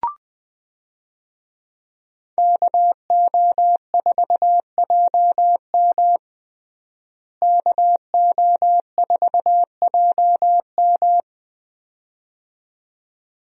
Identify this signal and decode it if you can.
Morse code, heard as a keyed tone, about 20 wpm, 700 Hz. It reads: KO4JM KO4JM
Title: USA Call Signs - 5 Characters - Repeated 2x - 20wpm